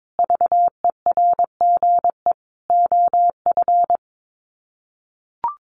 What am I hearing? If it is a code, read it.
Morse code, heard as a keyed tone, about 22 wpm, 700 Hz. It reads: VERGE OF
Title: Sets of 5 Words Encore 22wpm